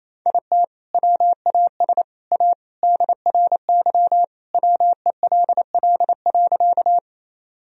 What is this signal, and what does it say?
Morse code, heard as a keyed tone, about 28 wpm, 700 Hz. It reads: IT WAS A DRY WELL.
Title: Sentences from Top 1000 Words - Review Part 3 - 28wpm